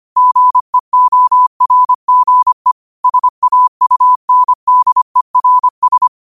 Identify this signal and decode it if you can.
Morse code, heard as a keyed tone, about 25 wpm, 1000 Hz. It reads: GEORGE SAUNDERS